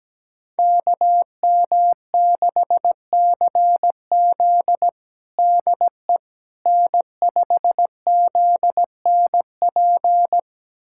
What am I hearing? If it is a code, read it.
Morse code, heard as a keyed tone, about 17 wpm, 700 Hz. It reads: KM6CZ DE N5ZNP
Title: USA Callsign DE Callsign - 5 Characters - 17wpm